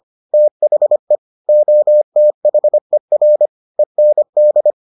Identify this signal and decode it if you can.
Morse code, heard as a keyed tone, about 25 wpm, 600 Hz. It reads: THE OTHER END